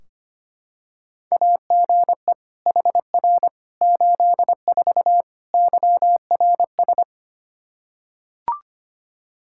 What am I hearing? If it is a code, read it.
Morse code, heard as a keyed tone, about 25 wpm, 700 Hz. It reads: AGE HR 84 YRS
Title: Practice QSO - Exchange 3 - More Info - 25wpm